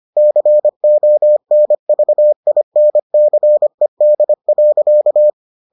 Morse code, heard as a keyed tone, about 25 wpm, 600 Hz. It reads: CONVINCED.